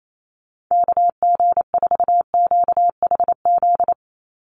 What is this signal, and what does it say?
Morse code, heard as a keyed tone, about 28 wpm, 700 Hz. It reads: KG4QHZ
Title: USA Call Signs 28wpm